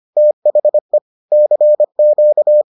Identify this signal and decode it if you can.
Morse code, heard as a keyed tone, about 25 wpm, 600 Hz. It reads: THE CQ